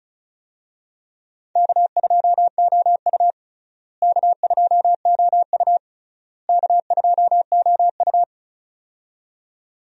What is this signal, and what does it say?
Morse code, heard as a keyed tone, about 36 wpm, 700 Hz. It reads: K2OU K2OU K2OU
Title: USA Call Signs - 4 Characters - Repeated 3x - 35wpm